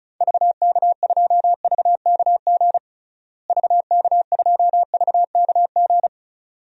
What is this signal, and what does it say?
Morse code, heard as a keyed tone, about 36 wpm, 700 Hz. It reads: VK2VKG VK2VKG
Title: DXCC-100 Call Signs - 6 Characters - Repeated 2x - 35wpm